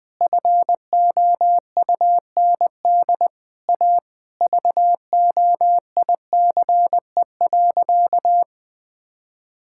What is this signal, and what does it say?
Morse code, heard as a keyed tone, about 20 wpm, 700 Hz. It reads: FOUND A VOICE.
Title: Sentences from Top 1600 Words 20wpm